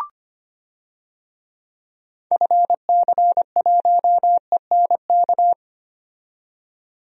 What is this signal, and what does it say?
Morse code, heard as a keyed tone, about 25 wpm, 700 Hz. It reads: FC1ENK
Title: DXCC-100 Call Signs - 6 Characters - 25wpm